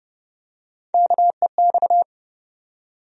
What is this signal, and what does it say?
Morse code, heard as a keyed tone, about 30 wpm, 700 Hz. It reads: KEX
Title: USA Call Sign Postfixes - 3 Letters 30wpm